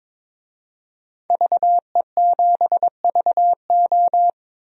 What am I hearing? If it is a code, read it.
Morse code, heard as a keyed tone, about 22 wpm, 700 Hz. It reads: VE7VO
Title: DXCC-100 Call Signs - 5 Characters - 22wpm